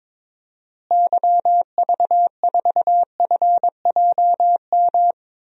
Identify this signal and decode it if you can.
Morse code, heard as a keyed tone, about 22 wpm, 700 Hz. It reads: YV4FJM
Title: DXCC-100 Call Signs - 6 Characters - Repeated 2x - 22wpm